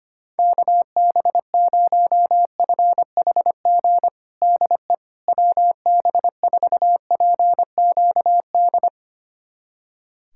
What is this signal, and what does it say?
Morse code, heard as a keyed tone, about 25 wpm, 700 Hz. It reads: KB0FHG DE WB4PQD
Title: USA Callsign DE Callsign - 6 Characters - 25wpm